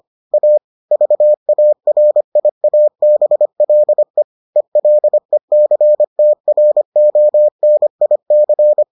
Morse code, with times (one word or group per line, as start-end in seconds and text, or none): 0.3-0.6 A
0.9-4.2 VARIABLE
4.6-8.8 ELECTRONIC